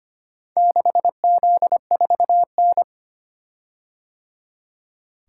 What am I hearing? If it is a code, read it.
Morse code, heard as a keyed tone, about 25 wpm, 700 Hz. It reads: 6Z4N
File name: practice / 25wpm / Four Letter-Number Combinations - 25wpm